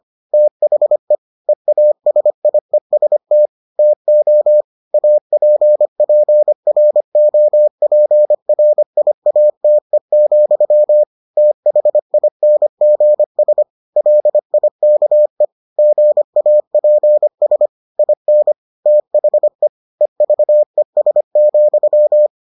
Morse code, heard as a keyed tone, about 25 wpm, 600 Hz. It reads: THE EASIEST TO APPROPRIATE, THINGS LIKE GAPS IN THE EVES,